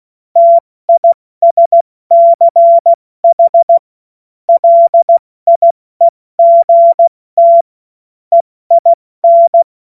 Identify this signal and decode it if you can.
Morse code, heard as a keyed tone, about 16 wpm, 680 Hz. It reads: TISCH LIEGT EIN
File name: Witz-21-Oktober-2025-70-F80